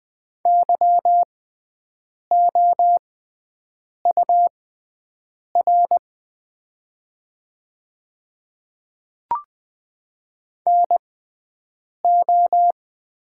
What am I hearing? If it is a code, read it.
Morse code, heard as a keyed tone, about 20 wpm, spaced as for 7 wpm, 700 Hz. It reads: YOUR NO
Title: 6x Character Spacing - Top 200 Words - 20wpm